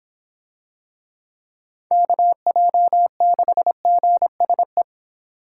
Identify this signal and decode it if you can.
Morse code, heard as a keyed tone, about 26 wpm, 700 Hz. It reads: KJ6GSE